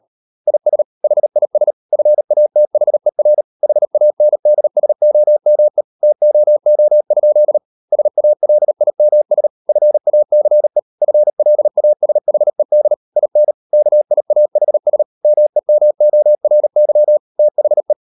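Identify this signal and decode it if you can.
Morse code, heard as a keyed tone, about 38 wpm, 600 Hz. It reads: IS HIS FATHER HANDSOME TOO? SALIMS FACE FLASHED IN KIAHS MEMORY THE
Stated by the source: Morse code practice at 38 WPM, sending Ken Follett's novel Never in English, from CW med Gnister.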